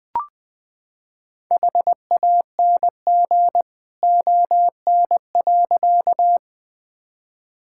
Morse code, heard as a keyed tone, about 20 wpm, 700 Hz. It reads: HANG ON.
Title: Sentences from Top 1600 Words 20wpm